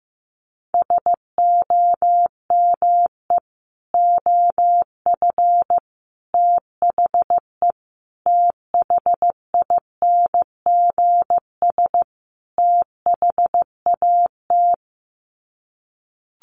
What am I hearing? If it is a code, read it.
Morse code, heard as a keyed tone, about 15 wpm, 700 Hz. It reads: SOME OF THE THINGS THAT
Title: Sets of 5 Words 15wpm